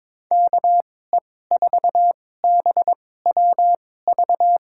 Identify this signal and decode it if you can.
Morse code, heard as a keyed tone, about 22 wpm, spaced as for 16 wpm, 700 Hz. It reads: KE4BWV
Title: USA Call Signs - 6 Characters - 2x Character Spacing - 22wpm